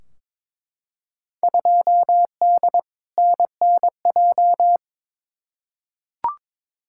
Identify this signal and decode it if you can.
Morse code, heard as a keyed tone, about 22 wpm, 700 Hz. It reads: 2D NNJ
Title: Field Day Exchanges - 22wpm